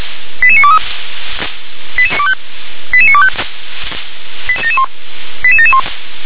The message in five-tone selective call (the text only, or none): 88923 88925 88771